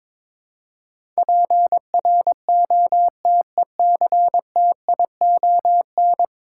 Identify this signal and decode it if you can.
Morse code, heard as a keyed tone, about 22 wpm, 700 Hz. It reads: PROTECTION
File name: practice / 22wpm / 10 Letter Words - First 3 Letters Given - 22wpm